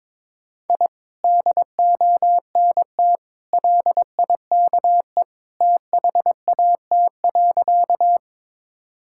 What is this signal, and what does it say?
Morse code, heard as a keyed tone, about 22 wpm, 700 Hz. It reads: I DONT LIKE THAT.
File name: practice / 22wpm / Sentences with 2-Word Phrases - 22wpm